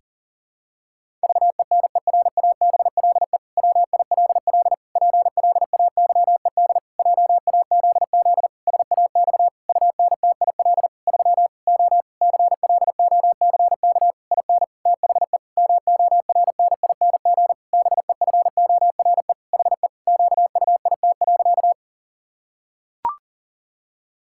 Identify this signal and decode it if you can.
Morse code, heard as a keyed tone, about 40 wpm, 700 Hz. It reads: VENERABLE WILL PLAYED JAZZ SAX UNTIL 3 O CLOCK IN THE MORNING BEFORE HE QUIT.
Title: Pangram Challenge v2 - 40wpm